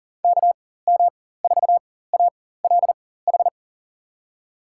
Morse code, heard as a keyed tone, about 37 wpm, spaced as for 18 wpm, 700 Hz. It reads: KM4ALH
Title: USA Call Signs - 6 Characters - 4x Character Spacing - 40wpm